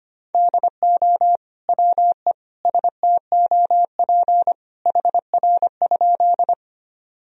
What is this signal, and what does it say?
Morse code, heard as a keyed tone, about 25 wpm, 700 Hz. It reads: DO WE STOP HR?
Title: Sentences from Top 300 Words and Abbreviations 25wpm